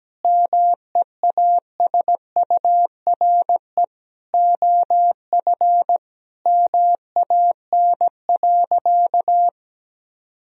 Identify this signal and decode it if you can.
Morse code, heard as a keyed tone, about 17 wpm, 700 Hz. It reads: MEASURE OF MAN.
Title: Sentences from Top 1000 Words - Review Part 2 - 17wpm